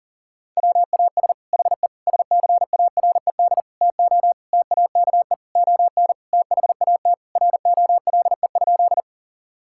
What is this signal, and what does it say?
Morse code, heard as a keyed tone, about 37 wpm, 700 Hz. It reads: WAS HE SCARED TO TAKE ON THAT ROLE?
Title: Sentences from Top 1000 Words - Review Part 6 - 40wpm